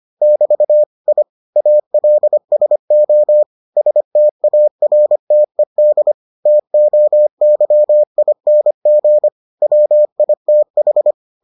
Morse code, heard as a keyed tone, about 25 wpm, 600 Hz. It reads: <BT> I ALSO STARTED TOYING WITH